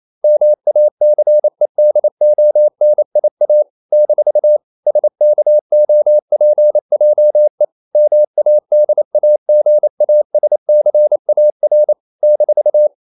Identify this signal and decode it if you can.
Morse code, heard as a keyed tone, about 28 wpm, 600 Hz. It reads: MACEDONIA - SKOPJE MADAGASCAR -
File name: CW / Practice / NationalCapitals28